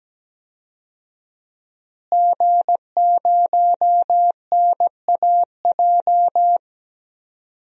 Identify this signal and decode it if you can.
Morse code, heard as a keyed tone, about 17 wpm, 700 Hz. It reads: G0NAJ
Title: DXCC-100 Call Signs - 5 Characters - Repeated 2x - 17wpm